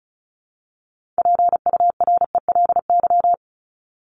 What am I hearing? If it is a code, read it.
Morse code, heard as a keyed tone, about 34 wpm, 700 Hz. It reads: PURELY
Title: LY Suffix - Top 100 Words with Base Words 35wpm